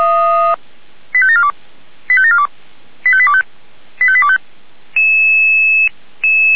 Five-tone selective call (ZVEI), heard as quick runs of five tones, tones of about 70 ms. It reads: 85742 85742 85825 85825
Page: SDR/ZVEI